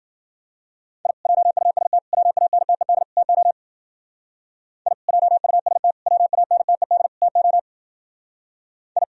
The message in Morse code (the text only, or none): I JUST WANTED TO I JUST WANTED TO I